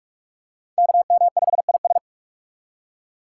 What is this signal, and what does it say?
Morse code, heard as a keyed tone, about 38 wpm, 700 Hz. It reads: KM5IS